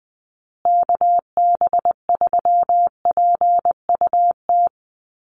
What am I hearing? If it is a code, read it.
Morse code, heard as a keyed tone, about 20 wpm, 700 Hz. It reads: KB3PUT
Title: USA Call Signs Repeated 3x 20wpm